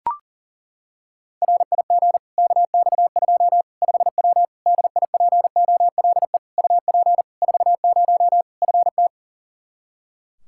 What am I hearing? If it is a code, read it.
Morse code, heard as a keyed tone, about 40 wpm, 700 Hz. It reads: RIG KX2 5W DIPOLE UP 40 FT
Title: Practice QSO - Exchange 2 - Info - 40wpm